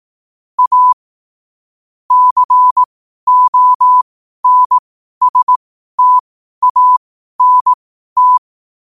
Morse code, heard as a keyed tone, about 18 wpm, spaced as for 12 wpm, 1000 Hz. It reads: A CONSTANT
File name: qotdfiles / 12wpm